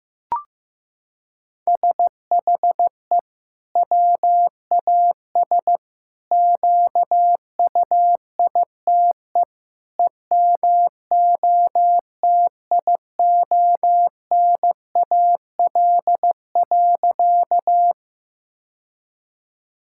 Morse code, heard as a keyed tone, about 16 wpm, 700 Hz. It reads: SHE WAS QUITE EMOTIONAL.
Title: Sentences from Top 1000 Words - Review Part 5 - 15wpm